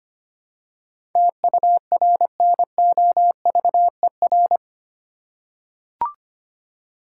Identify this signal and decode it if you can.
Morse code, heard as a keyed tone, about 25 wpm, 700 Hz. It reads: TURNOVER
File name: 8 Letter Words - First 3 Letters Given - 25wpm